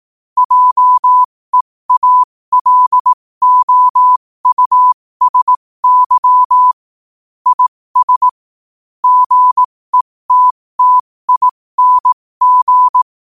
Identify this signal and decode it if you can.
Morse code, heard as a keyed tone, about 18 wpm, spaced as for 15 wpm, 1000 Hz. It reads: JEALOUSY IS GETTING
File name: qotdfiles / 15wpm